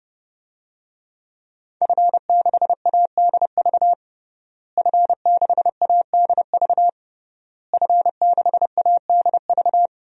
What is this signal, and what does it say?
Morse code, heard as a keyed tone, about 30 wpm, 700 Hz. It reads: F6ADV F6ADV F6ADV
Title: DXCC-100 Call Signs - 5 Characters - Repeated 3x - 30wpm